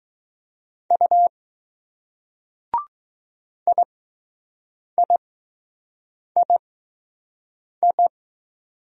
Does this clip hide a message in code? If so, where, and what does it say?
Yes: Morse code from 0.9 to 8.1 s, heard as a keyed tone, about 20 wpm, spaced as for 6 wpm, 700 Hz. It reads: U IIII